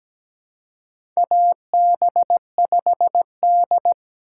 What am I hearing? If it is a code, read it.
Morse code, heard as a keyed tone, about 17 wpm, 700 Hz. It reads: AB5D
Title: USA Call Signs - 3x Character Spacing - 17wpm